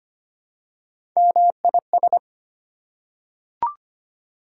Morse code, heard as a keyed tone, about 25 wpm, 700 Hz. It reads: MIS
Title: Prefixes and Suffixes 25wpm